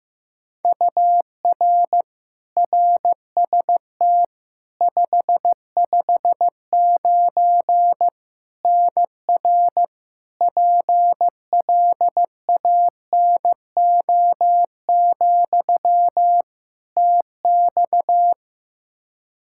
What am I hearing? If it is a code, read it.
Morse code, heard as a keyed tone, about 15 wpm, 700 Hz. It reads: UR RST 559 NR PLANO, TX
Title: Practice QSO - Exchange 1 - Beginning - 15wpm